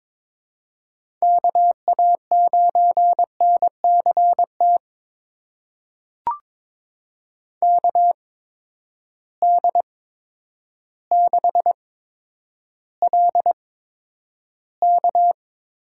Morse code, heard as a keyed tone, about 22 wpm, spaced as for 10 wpm, 700 Hz. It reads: KA9NCT KD6LK
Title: USA Call Signs - 8x Character Spacing - 22wpm